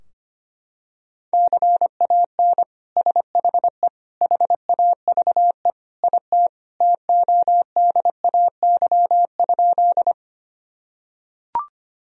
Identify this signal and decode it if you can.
Morse code, heard as a keyed tone, about 25 wpm, 700 Hz. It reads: CAN SHE HAVE IT TODAY?
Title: Sentences from Top 1000 Words - Review Part 4 - 25wpm